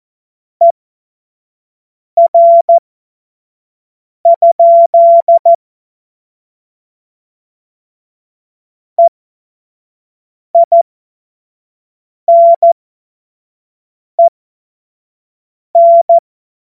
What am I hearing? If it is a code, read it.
Morse code, heard as a keyed tone, about 14 wpm, spaced as for 5 wpm, 680 Hz. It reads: ER? EINEN
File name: Morsekurs / Witze / Witz 02. Mai 2023 25 F70